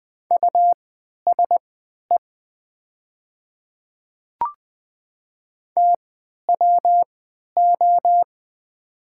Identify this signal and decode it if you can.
Morse code, heard as a keyed tone, about 20 wpm, spaced as for 11 wpm, 700 Hz. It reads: USE TWO
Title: 3x Character Spacing - Top 100 Words - 20wpm